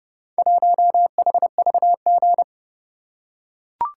Morse code, heard as a keyed tone, about 30 wpm, 700 Hz. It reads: 1HVG